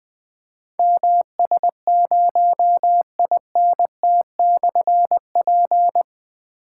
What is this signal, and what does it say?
Morse code, heard as a keyed tone, about 20 wpm, 700 Hz. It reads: MS0INT/P
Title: Difficult Call Signs 20wpm